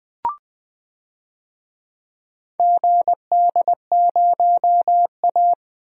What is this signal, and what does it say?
Morse code, heard as a keyed tone, about 20 wpm, 700 Hz. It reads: GD0A
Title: DXCC-100 Call Signs - 4 Characters - Repeated 3x - 20wpm